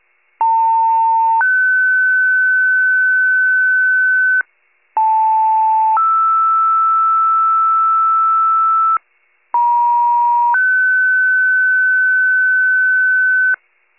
Two-tone page, heard as a one-second tone, then a three-second tone, about 900 then 1500 Hz, about 880 then 1320 Hz, about 950 then 1500 Hz. It is heard from 0.4 to 13.6 s.